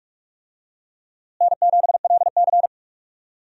Morse code, heard as a keyed tone, about 46 wpm, 700 Hz. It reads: N7LC
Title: USA Call Signs - 4 Characters - Repeated 2x - 45wpm